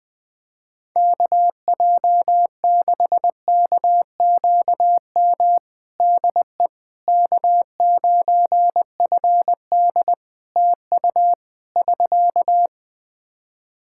Morse code, heard as a keyed tone, about 20 wpm, 700 Hz. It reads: KJ6KQM DE K9FD TU <SK>
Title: K1USN SST QSO Practice 20wpm